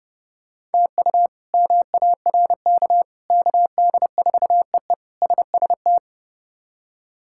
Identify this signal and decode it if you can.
Morse code, heard as a keyed tone, about 30 wpm, 700 Hz. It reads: TU MARK KD4EE SST
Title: K1USN SST QSO Practice - 30wpm